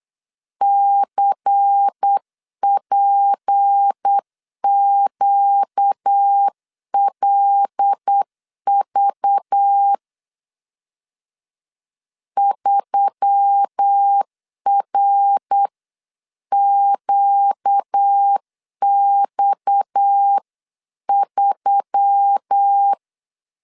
Morse code, heard as a keyed tone, about 8 wpm, 800 Hz. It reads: CPQLV 3RQX3
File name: LEZIONI / 7 GF9Z / 7 G 8wpm 3118